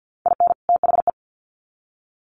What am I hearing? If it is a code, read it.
Morse code, heard as a keyed tone, about 50 wpm, 700 Hz. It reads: IN THE
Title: Sets of 2 Words 50wpm